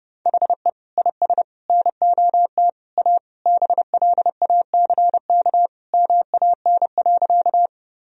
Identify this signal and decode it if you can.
Morse code, heard as a keyed tone, about 30 wpm, 700 Hz. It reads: HE IS NOT A BLACK MAN.
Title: Sentences from Top 1000 Words - Review Part 2 - 30wpm